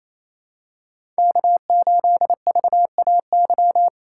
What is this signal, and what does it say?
Morse code, heard as a keyed tone, about 28 wpm, 700 Hz. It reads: K8VAY
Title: USA Call Signs - 5 Characters - 2x Character Spacing - 28wpm